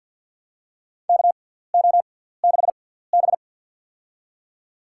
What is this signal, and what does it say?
Morse code, heard as a keyed tone, about 52 wpm, spaced as for 17 wpm, 700 Hz. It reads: KO6B
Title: USA Call Signs - 4 Characters - 6x Character Spacing - 50wpm